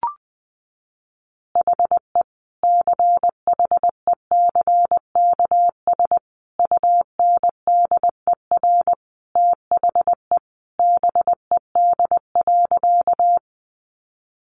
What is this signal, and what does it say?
Morse code, heard as a keyed tone, about 20 wpm, 700 Hz. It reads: HE CHECKS UNDER THE BED.
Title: Easy Phrases 20wpm